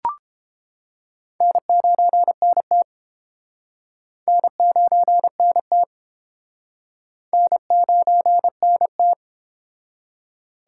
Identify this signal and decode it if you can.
Morse code, heard as a keyed tone, about 29 wpm, 700 Hz. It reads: N9NT N9NT N9NT